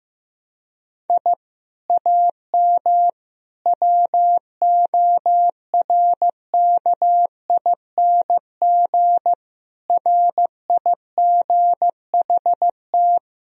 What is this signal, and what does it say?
Morse code, heard as a keyed tone, about 15 wpm, 700 Hz. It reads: I AM WORKING RIGHT
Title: Sentences from Top 1000 Words - Review Part 4 - 15wpm